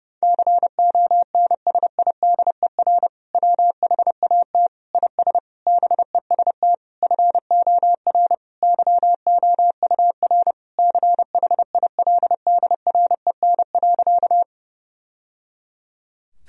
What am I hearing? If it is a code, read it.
Morse code, heard as a keyed tone, about 32 wpm, 700 Hz. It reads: CONSIDER WHAT IS BEST FOR YOUR CHILDREN.
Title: Sentences from Top 1000 Words - Review Part 4 - 30wpm